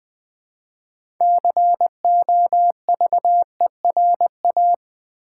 Morse code, heard as a keyed tone, about 20 wpm, 700 Hz. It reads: COVERA
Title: Build A Word - 2 Letters At A Time - 8-Letter Words - 20wpm